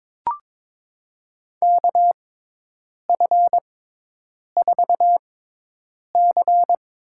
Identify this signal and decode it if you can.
Morse code, heard as a keyed tone, about 22 wpm, spaced as for 8 wpm, 700 Hz. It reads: KF4C
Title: USA Call Signs - 6 Characters - 6x Character Spacing - 22wpm